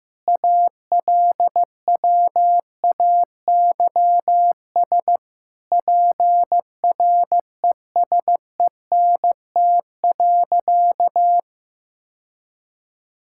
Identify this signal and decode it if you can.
Morse code, heard as a keyed tone, about 16 wpm, 700 Hz. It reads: ALWAYS PRESENT.
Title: Sentences from Top 1000 Words - Review Part 5 - 15wpm